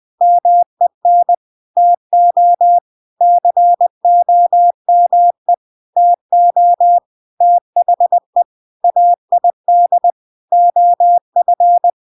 Morse code, heard as a keyed tone, about 20 wpm, 700 Hz. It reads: MEN TO COME TO THE AID OF
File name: Now is the time - 20wpm